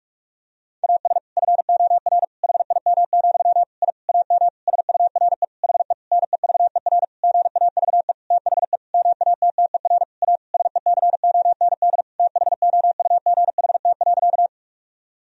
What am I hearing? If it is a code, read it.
Morse code, heard as a keyed tone, about 46 wpm, 700 Hz. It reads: AS FOR HIM, I AM SURE HE NEVER GAVE THE MATTER A SECOND THOUGHT.
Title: Sentences from Top 1000 Words - Review Part 2 - 45wpm